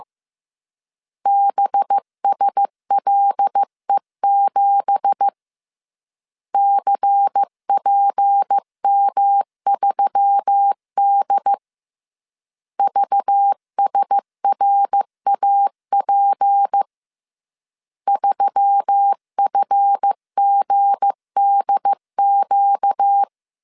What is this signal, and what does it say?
Morse code, heard as a keyed tone, about 15 wpm, 800 Hz. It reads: BSLE7 CPM3D VSRAP 3FGDQ
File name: CW / DET / 1 9 1500C S C 14wpm 3245